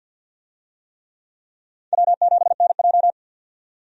Morse code, heard as a keyed tone, about 50 wpm, 700 Hz. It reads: W7NJ
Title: USA Call Signs - 4 Characters - 50wpm